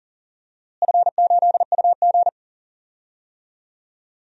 Morse code, heard as a keyed tone, about 40 wpm, 690 Hz. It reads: F8UG
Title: DXCC-100 Call Signs - 4 Characters - Repeated 2x - 40wpm